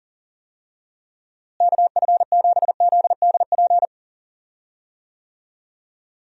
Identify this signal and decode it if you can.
Morse code, heard as a keyed tone, about 40 wpm, 700 Hz. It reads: KF7ZDP
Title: USA Call Signs - 6 Characters - Repeated 3x - 40wpm